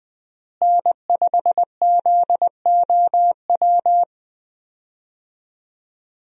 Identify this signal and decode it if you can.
Morse code, heard as a keyed tone, about 20 wpm, 700 Hz. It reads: N5ZOW
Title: USA Call Signs - 5 Characters - Repeated 2x - 20wpm